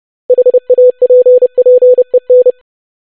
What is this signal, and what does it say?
Morse code, heard as a keyed tone, about 30 wpm, 500 Hz. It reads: HAPPEN
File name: basic500-25wpm